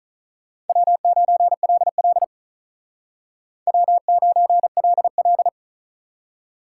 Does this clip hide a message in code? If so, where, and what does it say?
Yes: Morse code from 0.7 to 5.5 s, heard as a keyed tone, about 45 wpm, 700 Hz. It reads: W9LL W9LL